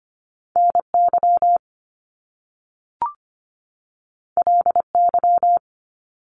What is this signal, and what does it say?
Morse code, heard as a keyed tone, about 25 wpm, 700 Hz. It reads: NY LY
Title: Top Two Letter Combinations in Words 25wpm